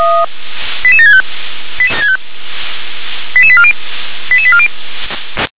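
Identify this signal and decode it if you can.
Five-tone selective call (ZVEI), heard as quick runs of five tones, tones of about 70 ms. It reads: 88765 88840 88840
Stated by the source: ZVEI five-tone selective call received by SDR.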